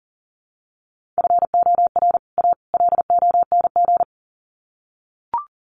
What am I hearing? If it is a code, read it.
Morse code, heard as a keyed tone, about 40 wpm, 700 Hz. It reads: FOR A LONG
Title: Sets of 3 Words 40wpm